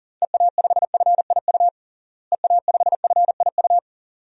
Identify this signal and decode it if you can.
Morse code, heard as a keyed tone, about 40 wpm, 700 Hz. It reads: EA5FIU EA5FIU
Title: DXCC-100 Call Signs - 6 Characters - Repeated 2x - 40wpm